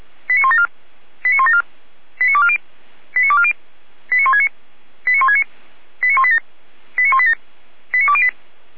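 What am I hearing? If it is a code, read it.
Five-tone selective call (ZVEI), heard as quick runs of five tones, tones of about 70 ms. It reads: 89174 89174 89240 89240 89169 89169 89187 89187 89208
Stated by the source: ZVEI five-tone selective call received on SDR.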